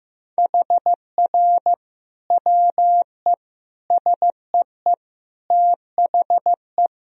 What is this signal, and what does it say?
Morse code, heard as a keyed tone, about 16 wpm, 700 Hz. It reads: HR WE SEE THE
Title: Sentences from Top 300 Words and Abbreviations 15wpm